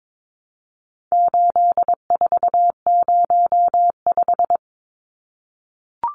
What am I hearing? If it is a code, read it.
Morse code, heard as a keyed tone, about 22 wpm, 700 Hz. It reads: 8405
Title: Numbers - 4-Digit 22wpm